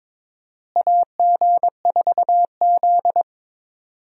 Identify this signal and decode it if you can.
Morse code, heard as a keyed tone, about 22 wpm, 700 Hz. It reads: AG4Z
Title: USA Call Signs - 4 Characters - Repeated 2x - 22wpm